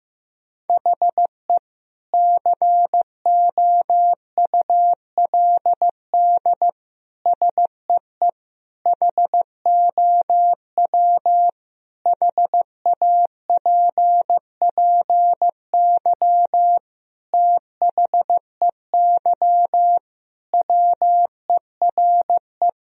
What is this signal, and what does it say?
Morse code, heard as a keyed tone, about 15 wpm, 700 Hz. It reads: HE COULD SEE HOW HAPPY THEY WERE
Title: Sentences from Top 1000 Words - Review Part 3 - 15wpm